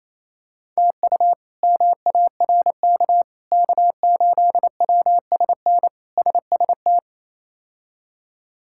Morse code, heard as a keyed tone, about 28 wpm, 700 Hz. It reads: TU MARK K8WSN SST
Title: K1USN SST QSO Practice - 28wpm